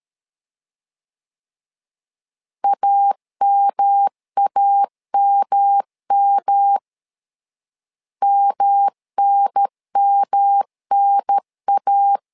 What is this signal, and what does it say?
Morse code, heard as a keyed tone, about 13 wpm, 800 Hz. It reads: AMAMM MNMNA
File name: LEZIONI / 3 MWRO / 3 M 12wpm 1541